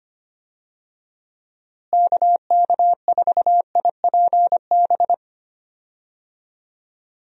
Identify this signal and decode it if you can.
Morse code, heard as a keyed tone, about 25 wpm, 700 Hz. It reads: KK4IPB
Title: USA Call Signs - 6 Characters - Repeated 2x - 25wpm